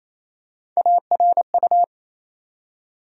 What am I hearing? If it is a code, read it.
Morse code, heard as a keyed tone, about 28 wpm, 700 Hz. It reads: ARU
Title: USA Call Sign Postfixes - 3 Letters 28wpm